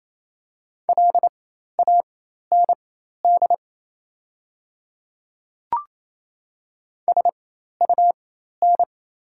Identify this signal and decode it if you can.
Morse code, heard as a keyed tone, about 28 wpm, spaced as for 13 wpm, 700 Hz. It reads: LAND SUN
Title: 4x Character Spacing - Top 300 Words - 28wpm